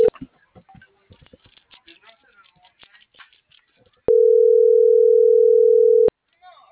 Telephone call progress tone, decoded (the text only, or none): ringback tone